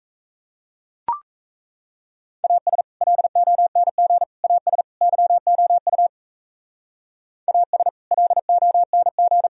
Morse code, heard as a keyed tone, about 34 wpm, 700 Hz. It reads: AS LONG AS YOU AS LONG
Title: Speed-Racing - Sets of 4 Words 28wpm